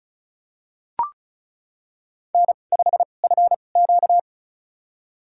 Morse code, heard as a keyed tone, about 36 wpm, spaced as for 25 wpm, 700 Hz. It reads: N5FQ
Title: USA Call Signs - 4 Characters - 2x Character Spacing - 35wpm